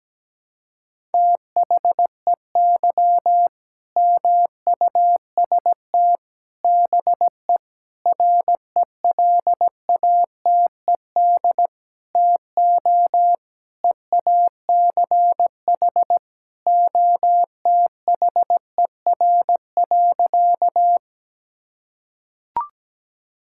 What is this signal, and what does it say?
Morse code, heard as a keyed tone, about 17 wpm, 700 Hz. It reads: THEY MUST BE RELATED TO EACH OTHER.
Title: Sentences from Top 1000 Words - Review Part 5 - 17wpm